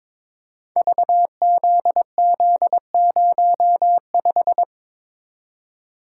Morse code, heard as a keyed tone, about 22 wpm, 700 Hz. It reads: VZZ05
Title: Code Groups - 5 Letter:Numbers - 22wpm